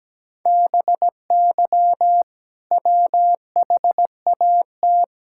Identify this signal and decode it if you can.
Morse code, heard as a keyed tone, about 17 wpm, 700 Hz. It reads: BY WHAT
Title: Sentences from Top 1000 Words - Review Part 6 - 17wpm